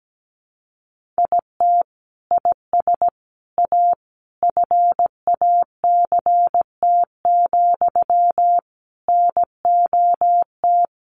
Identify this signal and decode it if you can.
Morse code, heard as a keyed tone, about 17 wpm, 700 Hz. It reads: IT IS A FACT, NOT
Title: Sentences from Top 1400 Words 17wpm